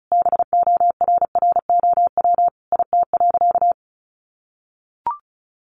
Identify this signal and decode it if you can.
Morse code, heard as a keyed tone, about 34 wpm, 700 Hz. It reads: BORROW IT.
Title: Sentences from Top 1500 Words 35wpm